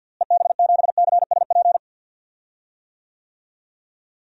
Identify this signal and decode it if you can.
Morse code, heard as a keyed tone, about 52 wpm, 700 Hz. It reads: EB7CSP
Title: DXCC-100 Call Signs - 6 Characters - Repeated 3x - 50wpm